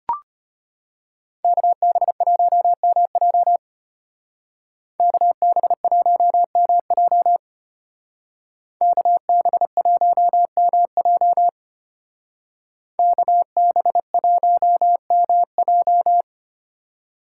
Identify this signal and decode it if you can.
Morse code, heard as a keyed tone, about 30 wpm, 700 Hz. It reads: KB1MJ KB1MJ KB1MJ KB1MJ
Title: Speed Racing - USA 2x2 Call Signs 25wpm